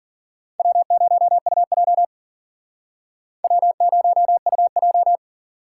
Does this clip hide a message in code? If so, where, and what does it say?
Yes: Morse code from 0.6 to 5.2 s, heard as a keyed tone, about 53 wpm, 700 Hz. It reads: W0UJ W0UJ